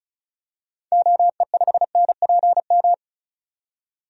Morse code, heard as a keyed tone, about 36 wpm, 700 Hz. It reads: OE5NPM
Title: DXCC-100 Call Signs - 6 Characters - 35wpm